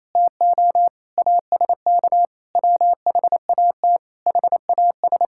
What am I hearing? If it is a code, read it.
Morse code, heard as a keyed tone, about 28 wpm, 700 Hz. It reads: TO ASK WHAT HAS
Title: Sentences from Top 1000 Words - Review Part 5 - 28wpm